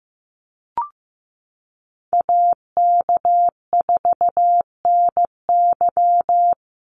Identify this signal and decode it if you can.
Morse code, heard as a keyed tone, about 15 wpm, 700 Hz. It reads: AK4NY
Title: Call Signs Repeated 2x - 15wpm